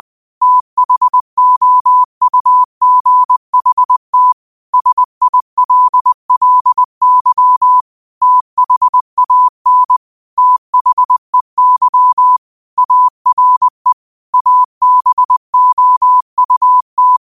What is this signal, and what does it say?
Morse code, heard as a keyed tone, about 20 wpm, 1000 Hz. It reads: THOUGHT SILLY THAN THEY ARE ABOUT
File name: qotdfiles / 20wpm